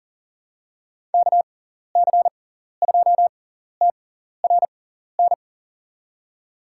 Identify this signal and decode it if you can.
Morse code, heard as a keyed tone, about 37 wpm, spaced as for 14 wpm, 700 Hz. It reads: KC2TRN
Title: USA Call Signs - 6 Characters - 6x Character Spacing - 40wpm